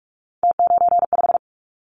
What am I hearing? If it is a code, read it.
Morse code, heard as a keyed tone, about 46 wpm, 700 Hz. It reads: T95
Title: Callsign Prefixes for Top 50 DXCC - 45wpm